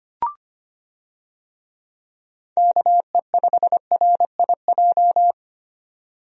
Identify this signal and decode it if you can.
Morse code, heard as a keyed tone, about 25 wpm, 700 Hz. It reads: KE5RIJ
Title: USA Call Signs - 6 Characters - 25wpm